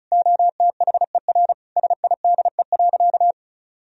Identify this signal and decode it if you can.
Morse code, heard as a keyed tone, about 34 wpm, 700 Hz. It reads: OTHER SIDE.
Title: Sentences from Top 1000 Words - Review Part 1 - 35wpm